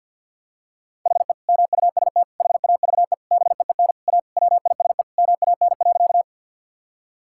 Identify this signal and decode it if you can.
Morse code, heard as a keyed tone, about 50 wpm, 700 Hz. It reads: HE MUST HAVE BEEN A WISE MAN.
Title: Sentences from Top 1600 Words 50wpm